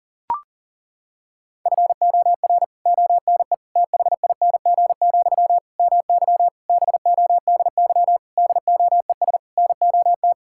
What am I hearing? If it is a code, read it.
Morse code, heard as a keyed tone, about 40 wpm, 700 Hz. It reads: FOR ONE THING, MY BODY DOES NOT
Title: Sentences from Top 1000 Words - Review Part 6 - 40wpm